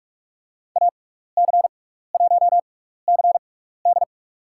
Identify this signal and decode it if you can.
Morse code, heard as a keyed tone, about 47 wpm, spaced as for 16 wpm, 690 Hz. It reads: AC1CD
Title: USA Call Signs - 5 Characters - 6x Character Spacing - 45wpm